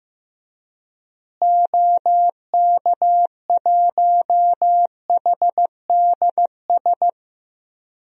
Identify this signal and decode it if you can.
Morse code, heard as a keyed tone, about 15 wpm, 700 Hz. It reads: OK1HDS
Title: DXCC-100 Call Signs - 6 Characters - Repeated 3x - 15wpm